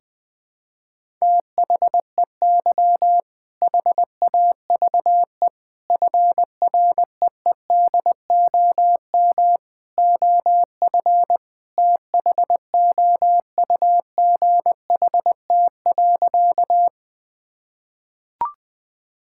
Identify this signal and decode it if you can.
Morse code, heard as a keyed tone, about 20 wpm, 700 Hz. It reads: THEY HAVE FREEDOM OF THOUGHT.
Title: Sentences from Top 1000 Words - Review Part 5 - 20wpm